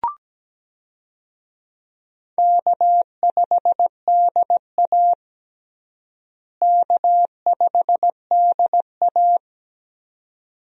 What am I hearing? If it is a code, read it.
Morse code, heard as a keyed tone, about 17 wpm, 700 Hz. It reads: K5DA K5DA
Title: USA Call Signs - 4 Characters - Repeated 2x - 17wpm